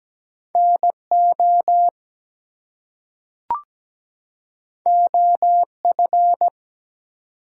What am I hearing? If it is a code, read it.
Morse code, heard as a keyed tone, about 17 wpm, 700 Hz. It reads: NO OF
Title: Two Letter Words - 17wpm